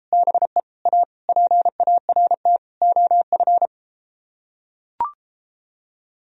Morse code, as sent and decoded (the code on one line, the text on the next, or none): -... / .- / .--. .- .-. - / --- ..-.
BE A PART OF